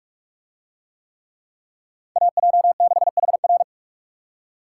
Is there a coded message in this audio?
Yes: Morse code from 2.2 to 3.6 s, heard as a keyed tone, about 44 wpm, 700 Hz. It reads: AJ6HR